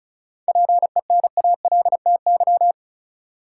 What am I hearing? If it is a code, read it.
Morse code, heard as a keyed tone, about 34 wpm, 700 Hz. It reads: PENALTY
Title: Seven Letter Words - 35wpm